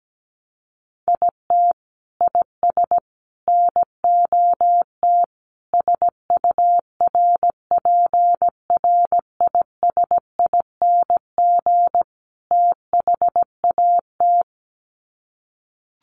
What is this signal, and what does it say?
Morse code, heard as a keyed tone, about 17 wpm, 700 Hz. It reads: IT IS NOT SURPRISING THAT
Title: Sets of 5 Words 17wpm